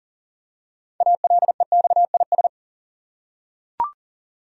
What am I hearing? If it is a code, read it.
Morse code, heard as a keyed tone, about 40 wpm, 700 Hz. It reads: ALEXIS
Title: USA Common Names - 40wpm